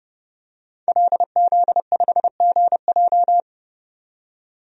Morse code, heard as a keyed tone, about 30 wpm, 700 Hz. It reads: LZ5GJ
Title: DXCC-100 Call Signs - 5 Characters - 30wpm